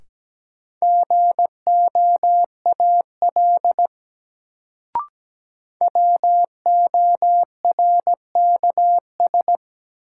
Morse code, heard as a keyed tone, about 17 wpm, 700 Hz. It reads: GOAL WORKS
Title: Sentences from Top 1000 Words - Review Part 5 - New Word Drill - 17wpm